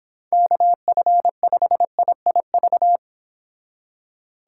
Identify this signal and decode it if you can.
Morse code, heard as a keyed tone, about 29 wpm, 700 Hz. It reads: KF5IIV